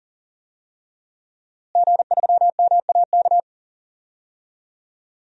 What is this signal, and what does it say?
Morse code, heard as a keyed tone, about 40 wpm, 700 Hz. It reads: G3MAK
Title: DXCC-100 Call Signs - 5 Characters - Repeated 2x - 40wpm